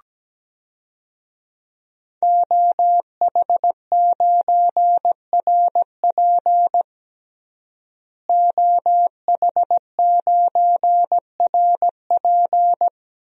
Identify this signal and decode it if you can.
Morse code, heard as a keyed tone, about 17 wpm, 700 Hz. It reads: OH9RP OH9RP
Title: DXCC-100 Call Signs - 5 Characters - Repeated 2x - 17wpm